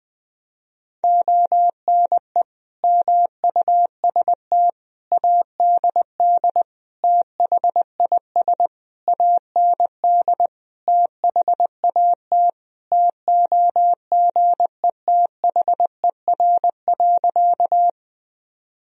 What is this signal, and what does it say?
Morse code, heard as a keyed tone, about 21 wpm, 700 Hz. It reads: ONE MUST ADD THIS AND THAT TOGETHER.
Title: Sentences from Top 1000 Words - Review Part 1 - 20wpm